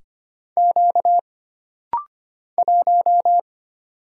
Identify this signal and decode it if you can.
Morse code, heard as a keyed tone, about 25 wpm, 700 Hz. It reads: Q 1
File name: Single Character (Letter:Number:Punctuation:Prosign) - Rapid-Fire - 25wpm